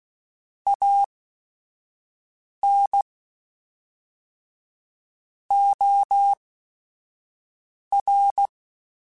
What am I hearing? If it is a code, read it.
Morse code, heard as a keyed tone, about 16 wpm, spaced as for 5 wpm, 800 Hz. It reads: AN OR